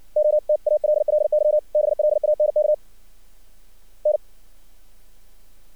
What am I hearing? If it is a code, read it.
Morse code, heard as a keyed tone, about 55 wpm, 600 Hz. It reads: OTALLY BLANK N
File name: archived / CRSnet